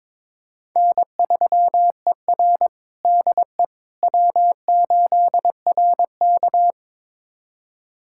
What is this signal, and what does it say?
Morse code, heard as a keyed tone, about 22 wpm, 700 Hz. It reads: N3ER DE W8RK
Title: USA Callsign DE Callsign - 4 Characters - 22wpm